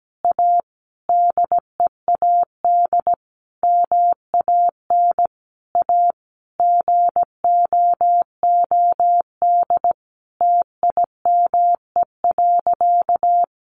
Morse code, heard as a keyed tone, about 17 wpm, 700 Hz. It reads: A DEAD MAN A GOOD TIME.
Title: Sentences from Top 1500 Words 17wpm